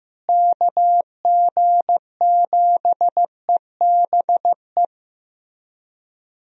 Morse code, heard as a keyed tone, about 15 wpm, 700 Hz. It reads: KG7EBE
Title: USA Call Signs - 6 Characters - Repeated 3x - 15wpm